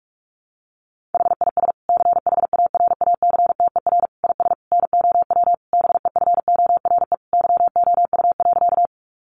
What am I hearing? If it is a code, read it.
Morse code, heard as a keyed tone, about 44 wpm, 700 Hz. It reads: HIS CHARACTER IS NOW BEFORE YOU.
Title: Sentences from Top 1200 Words 45wpm